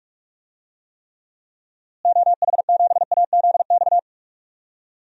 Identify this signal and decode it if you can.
Morse code, heard as a keyed tone, about 46 wpm, 700 Hz. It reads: OH7AZX
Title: DXCC-100 Call Signs - 6 Characters - Repeated 2x - 45wpm